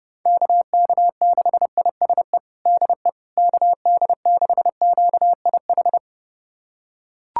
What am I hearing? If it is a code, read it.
Morse code, heard as a keyed tone, about 30 wpm, 700 Hz. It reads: KK6ISE DE KD6QIH
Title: USA Callsign DE Callsign - 6 Characters - 30wpm